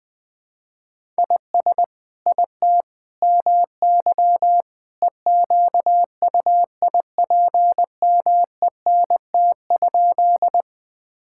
Morse code, heard as a keyed tone, about 21 wpm, 700 Hz. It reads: IS IT MY EQUIPMENT?
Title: Sentences from Top 1000 Words - Review Part 5 - 20wpm